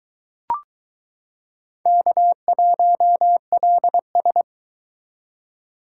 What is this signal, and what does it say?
Morse code, heard as a keyed tone, about 23 wpm, 700 Hz. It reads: K1LS